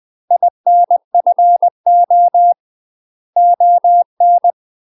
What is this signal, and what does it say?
Morse code, heard as a keyed tone, about 20 wpm, 700 Hz. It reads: INFO ON
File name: UK MASTER QSOs TRAINING / 20WPM / UK MASTER QSOs 0069